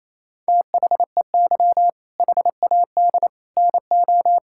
Morse code, heard as a keyed tone, about 27 wpm, 700 Hz. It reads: THEY HAD NO
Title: Sentences from Top 1000 Words - Review Part 6 - 28wpm